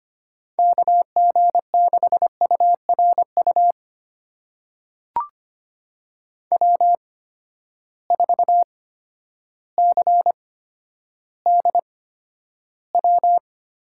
Morse code, heard as a keyed tone, about 25 wpm, spaced as for 11 wpm, 700 Hz. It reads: KG6URU W4CDW
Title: USA Call Signs - 8x Character Spacing - 25wpm